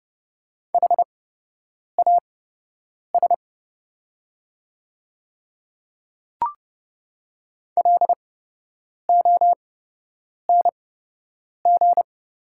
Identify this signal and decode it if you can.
Morse code, heard as a keyed tone, about 30 wpm, spaced as for 8 wpm, 700 Hz. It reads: HAS LONG
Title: 8x Character Spacing - Top 100 Words - 30wpm